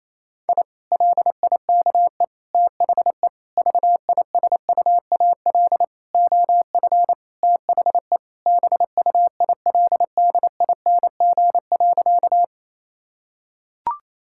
Morse code, heard as a keyed tone, about 28 wpm, 700 Hz. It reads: I LIKE THE VISUAL OF THE BUILDING.
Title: Sentences from Top 1000 Words - Review Part 6 - 28wpm